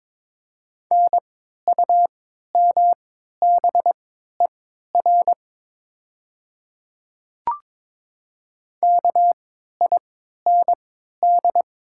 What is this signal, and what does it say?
Morse code, heard as a keyed tone, about 22 wpm, spaced as for 12 wpm, 700 Hz. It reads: NUMBER KIND
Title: 3x Character Spacing - Top 200 Words - 22wpm